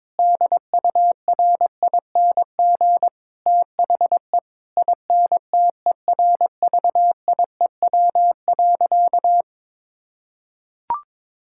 Morse code, heard as a keyed tone, about 22 wpm, 700 Hz. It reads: DURING THE INTERVIEW.
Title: Sentences from Top 1600 Words 22wpm